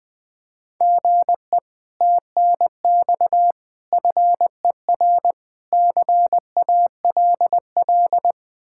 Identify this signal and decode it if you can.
Morse code, heard as a keyed tone, about 20 wpm, 700 Hz. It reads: GE TNX FER CALL
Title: Speed-Racing Common QSO Phrases - 20wpm